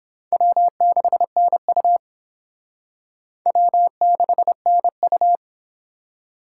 Morse code, heard as a keyed tone, about 26 wpm, 700 Hz. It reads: W6NU W6NU
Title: Speed Racing - USA 1x2 Call Signs 22wpm